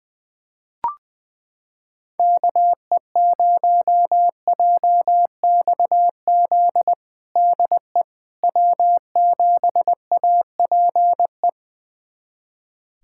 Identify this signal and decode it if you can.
Morse code, heard as a keyed tone, about 21 wpm, 700 Hz. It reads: KE0JXZ DE W7APE
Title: USA Callsign DE Callsign 20wpm